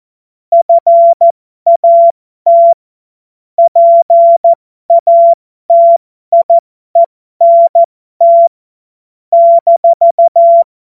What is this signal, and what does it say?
Morse code, heard as a keyed tone, about 14 wpm, spaced as for 12 wpm, 680 Hz. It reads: FAT PATIENT -
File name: Morsekurs / Witze